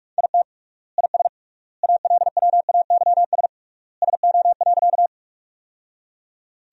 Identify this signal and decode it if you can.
Morse code, heard as a keyed tone, about 41 wpm, 700 Hz. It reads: IT IS ALWAYS SO.
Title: 3x Word-Spacing - Sentences from Top 300 Words - 45wpm